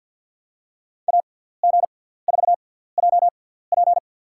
Morse code, heard as a keyed tone, about 50 wpm, spaced as for 17 wpm, 700 Hz. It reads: AG4JP
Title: USA Call Signs - 5 Characters - 6x Character Spacing - 50wpm